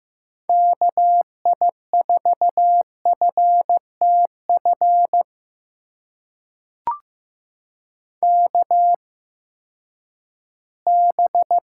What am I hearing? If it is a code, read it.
Morse code, heard as a keyed tone, about 15 wpm, spaced as for 10 wpm, 700 Hz. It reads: KI4FTF KB
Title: USA Call Signs - 6 Characters - 8x Character Spacing - 15wpm